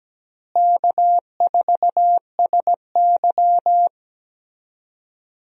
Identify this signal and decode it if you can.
Morse code, heard as a keyed tone, about 17 wpm, 700 Hz. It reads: K4SY
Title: USA Call Signs - 4 Characters - Repeated 2x - 17wpm